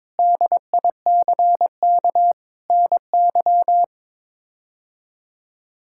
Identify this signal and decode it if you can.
Morse code, heard as a keyed tone, about 22 wpm, 700 Hz. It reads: DICK NY
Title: K1USN SST QSO Practice - 22wpm